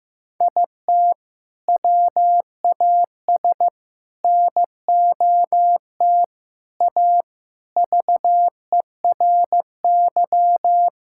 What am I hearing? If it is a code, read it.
Morse code, heard as a keyed tone, about 15 wpm, 700 Hz. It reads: IT WAS NOT A VERY
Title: Sentences from Top 1000 Words - Review Part 6 - 15wpm